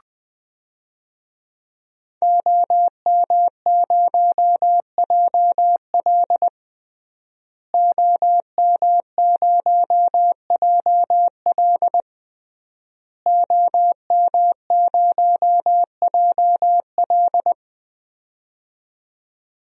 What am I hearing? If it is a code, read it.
Morse code, heard as a keyed tone, about 20 wpm, 700 Hz. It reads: OM0JL OM0JL OM0JL
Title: DXCC-100 Call Signs - 5 Characters - Repeated 3x - 20wpm